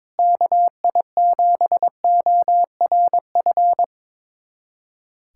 Morse code, heard as a keyed tone, about 22 wpm, 700 Hz. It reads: KI7ORF
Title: USA Callsign DE Callsign 22wpm